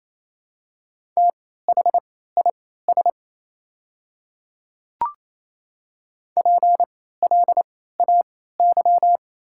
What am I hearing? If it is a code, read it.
Morse code, heard as a keyed tone, about 28 wpm, spaced as for 16 wpm, 700 Hz. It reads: THIS PLAY
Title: 3x Character Spacing - Top 200 Words - 28wpm